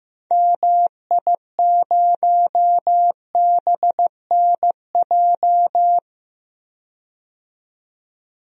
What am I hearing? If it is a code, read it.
Morse code, heard as a keyed tone, about 15 wpm, 700 Hz. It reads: MI0BNJ
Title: DXCC-100 Call Signs - 6 Characters - 15wpm